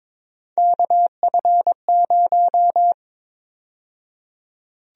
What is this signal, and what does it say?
Morse code, heard as a keyed tone, about 22 wpm, 700 Hz. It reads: KF0
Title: USA Call Sign Prefixes - 22wpm